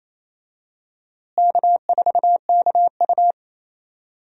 Morse code, heard as a keyed tone, about 28 wpm, 700 Hz. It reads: K4KU
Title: USA Call Signs - 4 Characters - Repeated 3x - 28wpm